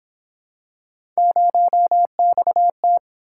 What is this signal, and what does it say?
Morse code, heard as a keyed tone, about 31 wpm, 700 Hz. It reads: 0XT